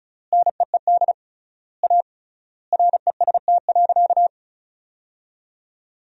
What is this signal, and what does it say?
Morse code, heard as a keyed tone, about 34 wpm, 700 Hz. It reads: NEED A REST.
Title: 3x Word-Spacing - Sentences from Top 500 Words - 35wpm